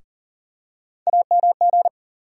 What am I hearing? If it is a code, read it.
Morse code, heard as a keyed tone, about 40 wpm, 700 Hz. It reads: AMG
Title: USA Call Sign Postfixes - 3 Letters 40wpm